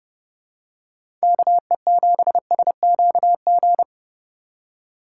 Morse code, heard as a keyed tone, about 30 wpm, 700 Hz. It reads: KE7SQG
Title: USA Call Signs - 6 Characters - 8x Character Spacing - 30wpm